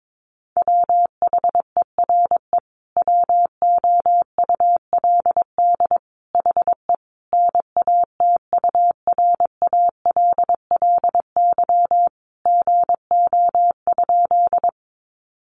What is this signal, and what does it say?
Morse code, heard as a keyed tone, about 22 wpm, 700 Hz. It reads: WHERE WOULD HE NATURALLY GO?
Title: Sentences from Top 1300 Words 22wpm